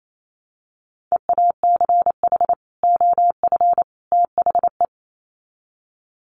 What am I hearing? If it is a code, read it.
Morse code, heard as a keyed tone, about 28 wpm, 700 Hz. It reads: EACH OF THE
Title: Sets of 3 Words 28wpm